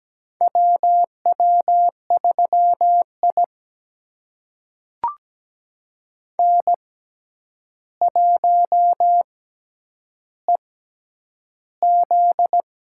Morse code, heard as a keyed tone, about 17 wpm, spaced as for 9 wpm, 700 Hz. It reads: WW3I N1EZ
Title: USA Call Signs - 4 Characters - 6x Character Spacing - 17wpm